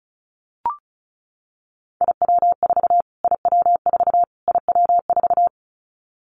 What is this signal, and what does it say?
Morse code, heard as a keyed tone, about 34 wpm, 700 Hz. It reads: IW4 IW4 IW4
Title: Callsign Prefixes for Top 25 DXCC - Repeated 3x - 35wpm